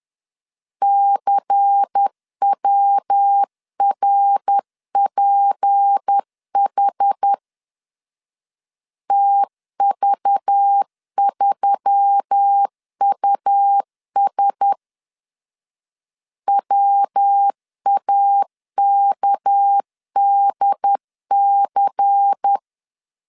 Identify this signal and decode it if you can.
Morse code, heard as a keyed tone, about 10 wpm, 800 Hz. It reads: CWRPH TV3US WAKDC
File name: LEZIONI / 6 PKX0 / 6 PK 10wpm 2436